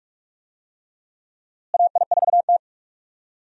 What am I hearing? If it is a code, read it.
Morse code, heard as a keyed tone, about 46 wpm, 700 Hz. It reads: AI4T
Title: USA Call Signs - 4 Characters - Repeated 3x - 45wpm